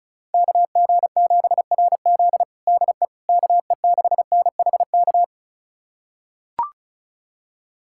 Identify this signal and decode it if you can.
Morse code, heard as a keyed tone, about 34 wpm, 690 Hz. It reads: KG7RZ DE KE6NHK
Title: USA Callsign DE Callsign 35wpm